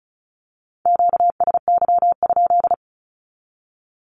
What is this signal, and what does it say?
Morse code, heard as a keyed tone, about 36 wpm, 700 Hz. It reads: QSY?
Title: Q-Codes with Spoken Meaning - 35wpm